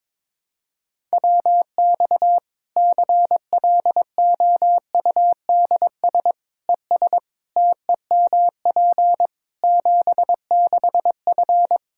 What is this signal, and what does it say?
Morse code, heard as a keyed tone, about 22 wpm, 700 Hz. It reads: WX CLOUDS ES TEMP 76F
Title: K1USN SST QSO Practice 22wpm